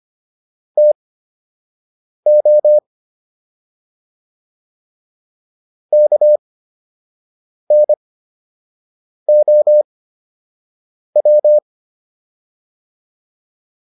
Morse code, heard as a keyed tone, about 25 wpm, spaced as for 6 wpm, 600 Hz. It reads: TO KNOW